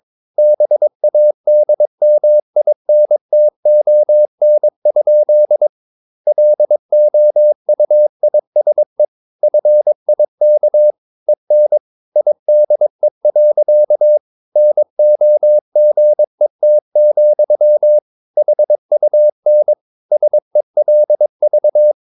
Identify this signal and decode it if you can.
Morse code, heard as a keyed tone, about 22 wpm, 600 Hz. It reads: BADMINTON? LOUISE FIK EN IDE. NOGET, HUN SELV
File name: Ferie QRS 21wpm